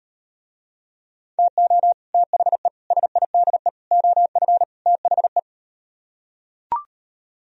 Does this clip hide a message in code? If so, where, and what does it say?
Yes: Morse code from 1.4 to 5.4 s, heard as a keyed tone, about 38 wpm, 700 Hz. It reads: TO THE SIDE OF THE